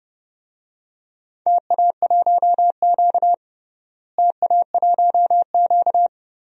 Morse code, heard as a keyed tone, about 30 wpm, 700 Hz. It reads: TA1Q TA1Q
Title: DXCC-100 Call Signs - 4 Characters - Repeated 2x - 30wpm